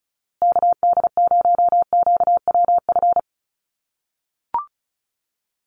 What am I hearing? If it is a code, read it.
Morse code, heard as a keyed tone, about 34 wpm, 700 Hz. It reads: KD0QWF
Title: USA Call Signs Repeated 3x 35wpm